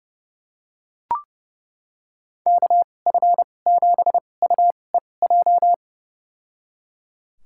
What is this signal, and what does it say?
Morse code, heard as a keyed tone, about 30 wpm, spaced as for 22 wpm, 700 Hz. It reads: KF7UEJ
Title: USA Call Signs - 6 Characters - 2x Character Spacing - 30wpm